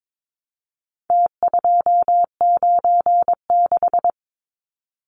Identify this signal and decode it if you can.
Morse code, heard as a keyed tone, about 22 wpm, 700 Hz. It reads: T296
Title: Numbers - Contest Style - 4-Digit 22wpm